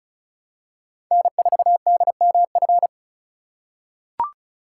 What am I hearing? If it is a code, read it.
Morse code, heard as a keyed tone, about 36 wpm, 700 Hz. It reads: N4DMF
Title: USA Call Signs - 5 Characters - 2x Character Spacing - 35wpm